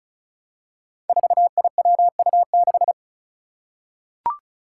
Morse code, heard as a keyed tone, about 36 wpm, 700 Hz. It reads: 4IWU6
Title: Five Challenging Letter-Number Combinations 35wpm